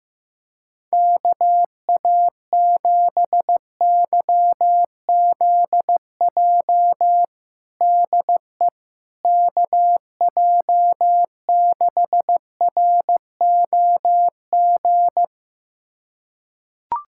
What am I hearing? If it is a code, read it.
Morse code, heard as a keyed tone, about 16 wpm, 700 Hz. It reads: KA7YZJ DE KJ6ROG
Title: USA Callsign DE Callsign - 6 Characters - 15wpm